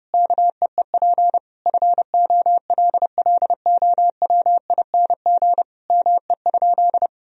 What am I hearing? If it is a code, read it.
Morse code, heard as a keyed tone, about 30 wpm, 700 Hz. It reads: KEEP FOLLOWING ME?
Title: Sentences from Top 1000 Words - Review Part 4 - 30wpm